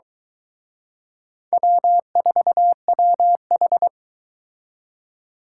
Morse code, heard as a keyed tone, about 22 wpm, 700 Hz. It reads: W4WH